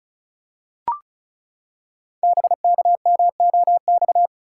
Morse code, heard as a keyed tone, about 36 wpm, 700 Hz. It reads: BKMOX
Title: Code Groups - 5 Letters - 35wpm